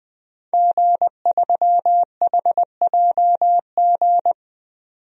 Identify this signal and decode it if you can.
Morse code, heard as a keyed tone, about 20 wpm, 700 Hz. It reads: G3HJG
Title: DXCC-100 Call Signs - 5 Characters - 20wpm